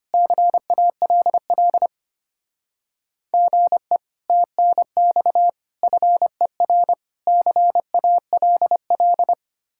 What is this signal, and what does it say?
Morse code, heard as a keyed tone, about 31 wpm, 700 Hz. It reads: CALL GE TNX FER CALL
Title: Speed-Racing Common QSO Phrases - 25wpm